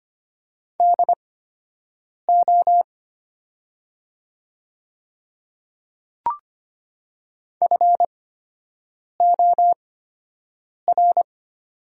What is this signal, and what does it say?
Morse code, heard as a keyed tone, about 25 wpm, spaced as for 7 wpm, 700 Hz. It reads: DO FOR
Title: 8x Character Spacing - Top 100 Words - 25wpm